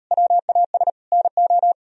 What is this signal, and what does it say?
Morse code, heard as a keyed tone, about 36 wpm, 700 Hz. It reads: WAS NO